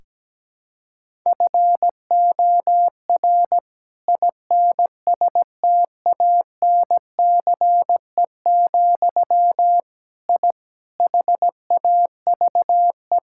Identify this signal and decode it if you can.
Morse code, heard as a keyed tone, about 17 wpm, 700 Hz. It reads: FOR INSTANCE, I HAVE
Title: Sentences from Top 1000 Words - Review Part 5 - 17wpm